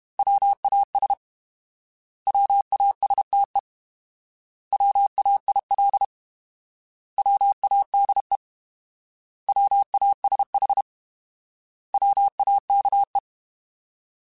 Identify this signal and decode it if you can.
Morse code, heard as a keyed tone, about 32 wpm, 800 Hz. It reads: WAS WASTE WAIL WADE WASH WAKE
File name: Alphabetical / Alphabetical mp3s / Ws @30